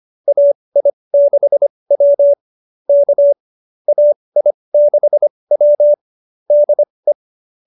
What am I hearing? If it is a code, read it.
Morse code, heard as a keyed tone, about 25 wpm, spaced as for 20 wpm, 590 Hz. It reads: AI6W K AI6W DE
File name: qso207b 20